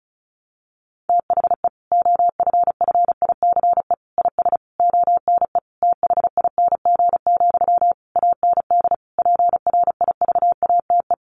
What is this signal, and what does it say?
Morse code, heard as a keyed tone, about 34 wpm, 700 Hz. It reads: THE OFFICE IS ONE THING, AND PRIVATE
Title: Sentences from Top 1000 Words 35wpm